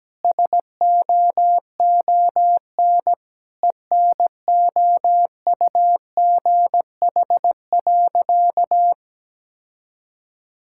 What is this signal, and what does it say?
Morse code, heard as a keyed tone, about 17 wpm, 700 Hz. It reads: SOON ENOUGH.
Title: Sentences from Top 1000 Words - Review Part 6 - 17wpm